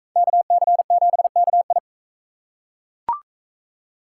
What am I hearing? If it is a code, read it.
Morse code, heard as a keyed tone, about 38 wpm, 700 Hz. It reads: KC7KI